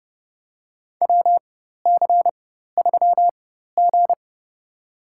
Morse code, heard as a keyed tone, about 31 wpm, spaced as for 14 wpm, 700 Hz. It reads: WC3G
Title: USA Call Signs - 4 Characters - 4x Character Spacing - 30wpm